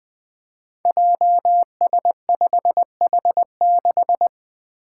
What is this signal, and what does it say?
Morse code, heard as a keyed tone, about 20 wpm, 700 Hz. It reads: JS5H6
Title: Five Challenging Letter-Number Combinations 20wpm